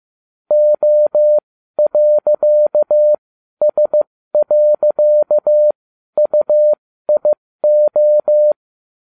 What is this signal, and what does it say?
Morse code, heard as a keyed tone, about 15 wpm, spaced as for 12 wpm, 600 Hz. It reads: O.S.UIO
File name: cw / ubacw12 2